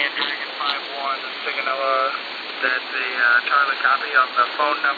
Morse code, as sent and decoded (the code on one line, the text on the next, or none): none